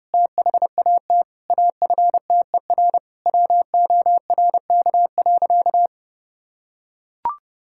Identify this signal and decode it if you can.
Morse code, heard as a keyed tone, about 30 wpm, 700 Hz. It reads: THAT AFTER WORK.
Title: Sentences from Top 1000 Words - Review Part 5 - 30wpm